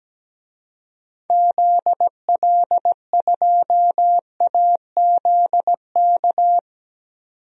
Morse code, heard as a keyed tone, about 17 wpm, 700 Hz. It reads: ZL2AZK
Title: DXCC-100 Call Signs - 6 Characters - 17wpm